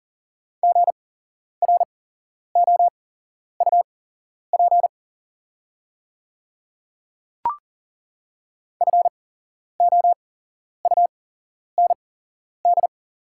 Morse code, heard as a keyed tone, about 40 wpm, spaced as for 11 wpm, 690 Hz. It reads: GROUP FOUND
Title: 8x Character Spacing - Top 500 Words - 40wpm